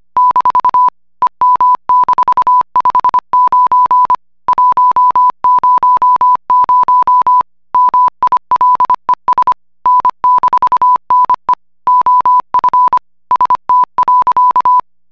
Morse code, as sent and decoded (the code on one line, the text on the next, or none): -....- / . -- -....- ..... ----. / .---- ----- ----- / -- .. .-.. ... / -. -....- -. / --- ..-. / ... - .-.-.-
- EM-59 100 MILES N-NE OF ST.